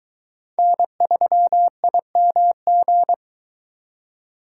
Morse code, heard as a keyed tone, about 22 wpm, 700 Hz. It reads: N3IMG